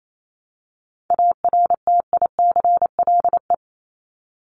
Morse code, heard as a keyed tone, about 28 wpm, 700 Hz. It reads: ARTICLE
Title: Top 751-1000 Words 28wpm